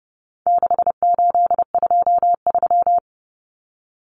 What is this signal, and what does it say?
Morse code, heard as a keyed tone, about 30 wpm, 700 Hz. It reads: 6823
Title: Numbers - 4-Digit 30wpm